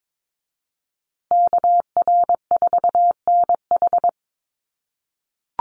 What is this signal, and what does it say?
Morse code, heard as a keyed tone, about 22 wpm, 700 Hz. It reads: KR4NH
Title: USA Call Signs Repeated 3x 22wpm